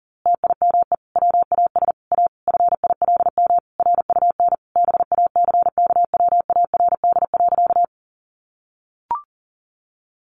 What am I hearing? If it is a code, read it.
Morse code, heard as a keyed tone, about 40 wpm, 700 Hz. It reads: TIME WAS A FILM RUN BACKWARD.
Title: A Sound Of Thunder 40wpm